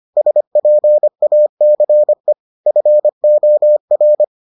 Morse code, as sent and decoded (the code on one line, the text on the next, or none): ... .--. .- -.-. / ..-. --- .-.
SPACE FOR